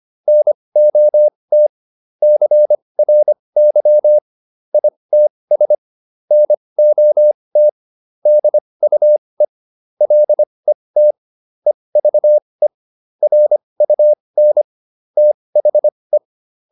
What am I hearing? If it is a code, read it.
Morse code, heard as a keyed tone, about 25 wpm, spaced as for 20 wpm, 600 Hz. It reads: NOT CRY ITS NOT DUE LET EVE RUN THE